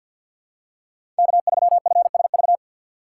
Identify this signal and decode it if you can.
Morse code, heard as a keyed tone, about 53 wpm, 700 Hz. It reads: K3FSV
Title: USA Call Signs - 5 Characters - 4x Character Spacing - 50wpm